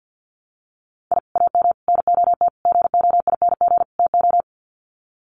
Morse code, heard as a keyed tone, about 50 wpm, 700 Hz. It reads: I AM NOT GOING TO
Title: Sets of 5 Words Encore 50wpm